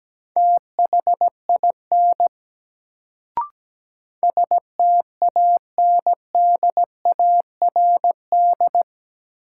Morse code, heard as a keyed tone, about 17 wpm, 700 Hz. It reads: THIN STANDARD
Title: Sentences from Top 1200 Words - New Word Drill Alt - 17wpm